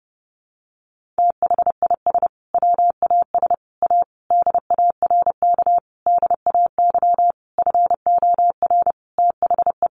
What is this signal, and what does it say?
Morse code, heard as a keyed tone, about 30 wpm, 700 Hz. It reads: THIS WAS A DARK DAY FOR THE
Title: Sentences from Top 1300 Words 30wpm